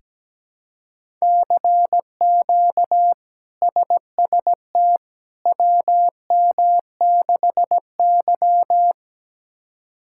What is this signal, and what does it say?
Morse code, heard as a keyed tone, about 17 wpm, 700 Hz. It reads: CQ SST WM6Y
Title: K1USN SST QSO Practice - 17wpm